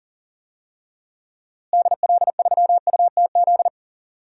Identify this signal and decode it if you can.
Morse code, heard as a keyed tone, about 40 wpm, 700 Hz. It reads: DL3UTZ
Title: DXCC-100 Call Signs - 6 Characters - Repeated 3x - 40wpm